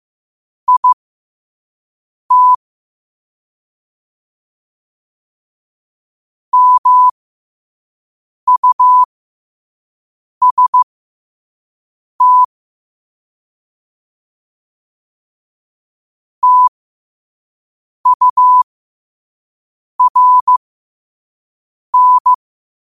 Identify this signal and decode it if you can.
Morse code, heard as a keyed tone, about 15 wpm, spaced as for 5 wpm, 1000 Hz. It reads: IT MUST TURN
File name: qotdfiles / 5wpm